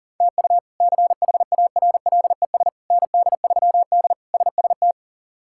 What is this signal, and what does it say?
Morse code, heard as a keyed tone, about 40 wpm, 700 Hz. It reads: TU CHARLES ND3D SST
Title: K1USN SST QSO Practice - 40wpm